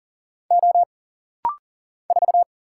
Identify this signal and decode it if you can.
Morse code, heard as a keyed tone, about 39 wpm, 700 Hz. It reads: O 4
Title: Single Letter-Number - Rapid-Fire 40wpm